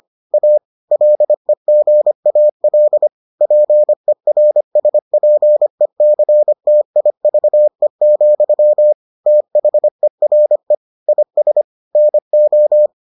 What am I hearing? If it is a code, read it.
Morse code, heard as a keyed tone, about 25 wpm, 600 Hz. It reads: A LEGAL PERSPECTIVE, THERE IS NO